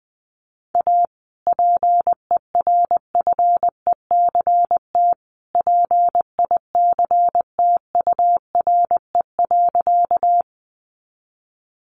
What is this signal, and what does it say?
Morse code, heard as a keyed tone, about 20 wpm, 700 Hz. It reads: A PERFECT PICTURE.
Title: Sentences from Top 1000 Words 20wpm